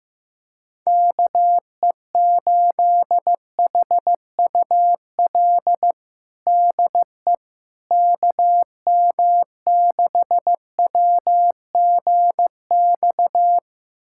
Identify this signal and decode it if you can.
Morse code, heard as a keyed tone, about 15 wpm, 700 Hz. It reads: KE8HUL DE KM6WGX
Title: USA Callsign DE Callsign 15wpm